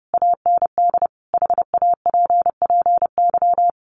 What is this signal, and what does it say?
Morse code, heard as a keyed tone, about 30 wpm, 700 Hz. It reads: AND HAPPY
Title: Sentences from Top 1000 Words 30wpm